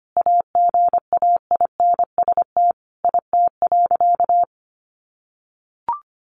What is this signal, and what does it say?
Morse code, heard as a keyed tone, about 25 wpm, 700 Hz. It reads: AGAINST IT.
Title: Sentences from Top 1400 Words 25wpm